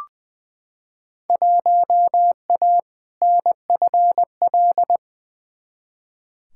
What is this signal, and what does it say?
Morse code, heard as a keyed tone, about 20 wpm, 700 Hz. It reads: 1A NFL
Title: Field Day Exchanges - 20wpm